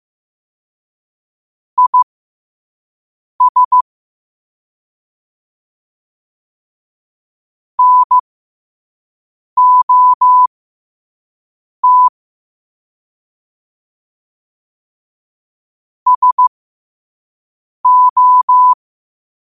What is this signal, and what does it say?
Morse code, heard as a keyed tone, about 15 wpm, spaced as for 5 wpm, 1000 Hz. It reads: IS NOT SO
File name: qotdfiles / 5wpm